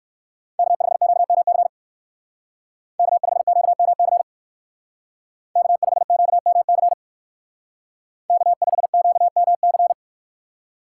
Morse code, heard as a keyed tone, about 54 wpm, 700 Hz. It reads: K5QMC K5QMC K5QMC K5QMC
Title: Speed Racing - USA 1x3 Call Signs 45wpm